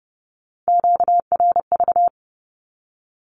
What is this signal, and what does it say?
Morse code, heard as a keyed tone, about 30 wpm, 700 Hz. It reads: QRV
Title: Q-Codes with Spoken Meaning - 30wpm